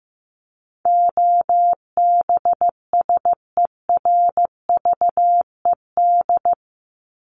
Morse code, heard as a keyed tone, about 14 wpm, 700 Hz. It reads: OBSERVED
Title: ED Suffix - Top 100 Words with Base Words 15wpm